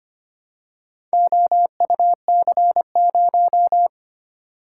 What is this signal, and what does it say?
Morse code, heard as a keyed tone, about 25 wpm, 700 Hz. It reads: OUC0
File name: practice / 25wpm / Four Letter-Number Combinations - 25wpm